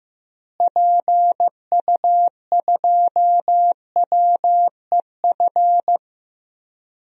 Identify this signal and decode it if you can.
Morse code, heard as a keyed tone, about 15 wpm, 700 Hz. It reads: PU2WEF
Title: DXCC-100 Call Signs - 6 Characters - Repeated 3x - 15wpm